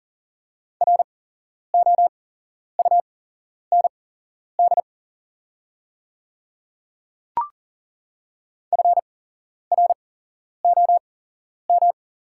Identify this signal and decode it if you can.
Morse code, heard as a keyed tone, about 40 wpm, spaced as for 11 wpm, 700 Hz. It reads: ROUND FROM
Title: 8x Character Spacing - Top 200 Words - 40wpm